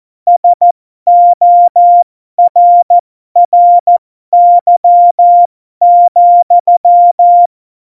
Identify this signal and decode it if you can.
Morse code, heard as a keyed tone, about 14 wpm, spaced as for 12 wpm, 700 Hz. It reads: SORRY,
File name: Morsekurs / Witze